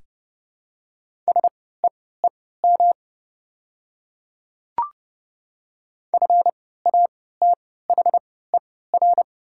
Morse code, heard as a keyed tone, about 31 wpm, spaced as for 17 wpm, 700 Hz. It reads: SEEM FATHER
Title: 3x Character Spacing - Top 300 Words - 30wpm